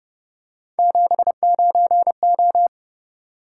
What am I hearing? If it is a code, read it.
Morse code, heard as a keyed tone, about 30 wpm, 700 Hz. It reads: 79O
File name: Speed-Racing - Three Letter-Number Combinations - 25wpm